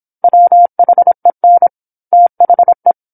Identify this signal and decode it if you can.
Morse code, heard as a keyed tone, about 27 wpm, 700 Hz. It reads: WHEN THE
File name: 08-Chapter-08 26 WPM